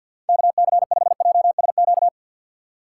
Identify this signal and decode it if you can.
Morse code, heard as a keyed tone, about 54 wpm, 700 Hz. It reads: KC5JSQ
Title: USA Call Signs - 6 Characters - 6x Character Spacing - 50wpm